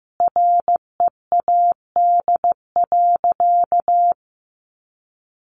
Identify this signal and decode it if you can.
Morse code, heard as a keyed tone, about 15 wpm, 700 Hz. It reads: READ.
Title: Sentences from Top 1400 Words 15wpm